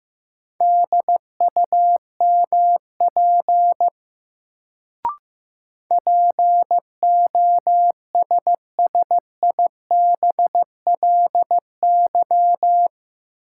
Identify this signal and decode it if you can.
Morse code, heard as a keyed tone, about 15 wpm, 700 Hz. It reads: DUMP POSSIBLY
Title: Sentences from Top 1300 Words - New Word Drill - 15wpm